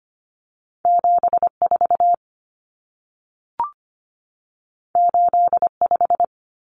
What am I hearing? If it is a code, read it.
Morse code, heard as a keyed tone, about 25 wpm, 700 Hz. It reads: 74 85
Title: Numbers - 2-Digit 25wpm